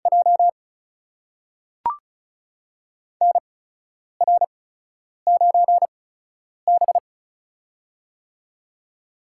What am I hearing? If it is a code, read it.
Morse code, heard as a keyed tone, about 36 wpm, spaced as for 10 wpm, 700 Hz. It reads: J NR9B
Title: USA Call Signs - 4 Characters - 8x Character Spacing - 35wpm